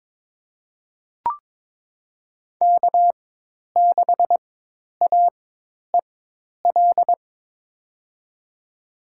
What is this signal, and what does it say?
Morse code, heard as a keyed tone, about 22 wpm, spaced as for 10 wpm, 700 Hz. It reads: K6AEL
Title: USA Call Signs - 5 Characters - 4x Character Spacing - 22wpm